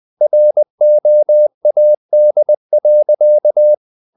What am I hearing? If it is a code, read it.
Morse code, heard as a keyed tone, about 20 wpm, 600 Hz. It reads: ROAD.
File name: the old man and the sea / the old man and the sea part 1 20wpm